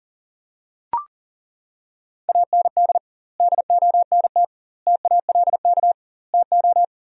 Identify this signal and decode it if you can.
Morse code, heard as a keyed tone, about 40 wpm, 700 Hz. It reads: AND DONT TALK TO
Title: 2x Word-Spacing - Sentences from Top 400 Words - 40wpm